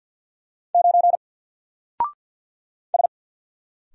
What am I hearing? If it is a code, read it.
Morse code, heard as a keyed tone, about 47 wpm, 700 Hz. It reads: 9 S
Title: Single Letter-Number Alt 50wpm